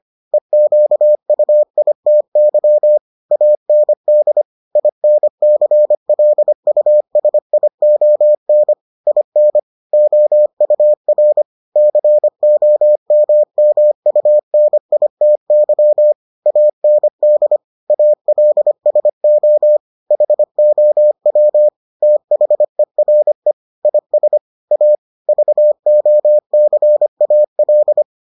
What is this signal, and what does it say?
Morse code, heard as a keyed tone, about 25 wpm, 600 Hz. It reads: EQUITY AND INCLUSION IN OUR COMMUNITY AND ALSO HOW THERE IS A VOCAL